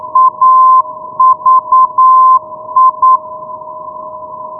Morse code, heard as a keyed tone, about 9 wpm, 1040 Hz. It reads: AVI
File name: europe / i-665 ndb aviano avi-390 rb 2010 ve